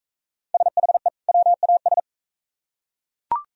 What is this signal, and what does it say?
Morse code, heard as a keyed tone, about 39 wpm, 700 Hz. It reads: SHE WAS